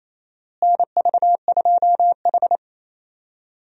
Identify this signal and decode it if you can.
Morse code, heard as a keyed tone, about 28 wpm, 700 Hz. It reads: NV2H
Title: USA Call Signs - 4 Characters - Repeated 2x - 28wpm